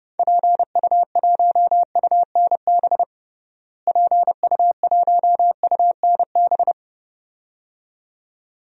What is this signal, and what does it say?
Morse code, heard as a keyed tone, about 31 wpm, 700 Hz. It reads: PU1UNB PU1UNB
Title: DXCC-100 Call Signs - 6 Characters - Repeated 2x - 30wpm